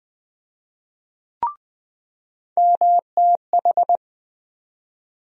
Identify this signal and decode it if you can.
Morse code, heard as a keyed tone, about 20 wpm, 700 Hz. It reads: MTH
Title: USA Call Sign Postfixes - 3 Letters 20wpm